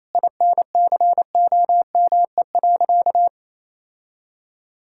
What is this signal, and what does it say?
Morse code, heard as a keyed tone, about 27 wpm, 700 Hz. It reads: INCOME.
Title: Sentences from Top 1000 Words - Review Part 5 - 28wpm